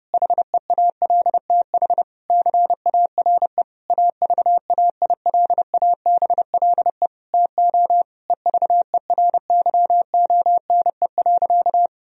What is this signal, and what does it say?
Morse code, heard as a keyed tone, about 28 wpm, 700 Hz. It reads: HEALTH CARE AVAILABLE TO EVERYONE.
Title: Sentences from Top 1000 Words - Review Part 3 - 30wpm